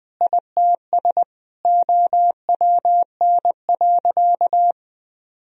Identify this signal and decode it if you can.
Morse code, heard as a keyed tone, about 20 wpm, 700 Hz. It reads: ITS OWN.
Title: Sentences from Top 1000 Words - Review Part 4 - 20wpm